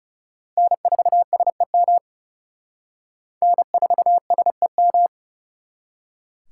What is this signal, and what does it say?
Morse code, heard as a keyed tone, about 36 wpm, 700 Hz. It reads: N4SEM N4SEM
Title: Speed Racing - USA 1x3 Call Signs 30wpm